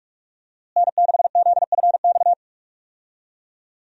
Morse code, heard as a keyed tone, about 45 wpm, 700 Hz. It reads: N6ZFX
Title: USA Call Signs - 5 Characters - Repeated 3x - 45wpm